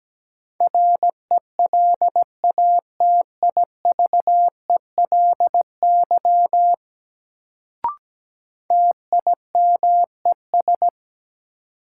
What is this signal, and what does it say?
Morse code, heard as a keyed tone, about 17 wpm, 700 Hz. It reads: RELATIVELY TIMES
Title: Sentences from Top 1100 Words - New Word Drill Alt - 17wpm